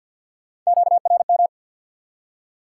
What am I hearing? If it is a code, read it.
Morse code, heard as a keyed tone, about 69 wpm, spaced as for 59 wpm, 700 Hz. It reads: QRM
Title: Speed-Racing Top 180 QSO Elements 50wpm